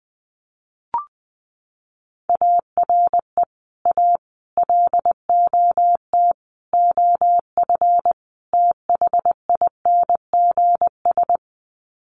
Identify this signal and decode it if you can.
Morse code, heard as a keyed tone, about 20 wpm, 700 Hz. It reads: ARE A LOT OF THINGS
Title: Sets of 5 Words Encore2 20wpm